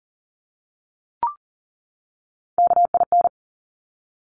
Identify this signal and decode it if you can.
Morse code, heard as a keyed tone, about 40 wpm, 700 Hz. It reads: KIN
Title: Top Three Letter Combinations in Words 40wpm